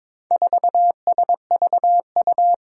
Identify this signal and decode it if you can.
Morse code, heard as a keyed tone, about 22 wpm, 700 Hz. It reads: 4SVU
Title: Four Challenging Letter-Number Combinations 22wpm